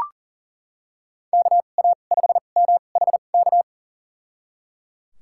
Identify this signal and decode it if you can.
Morse code, heard as a keyed tone, about 37 wpm, spaced as for 28 wpm, 700 Hz. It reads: KA5MHK
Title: USA Call Signs - 2x Character Spacing - 40wpm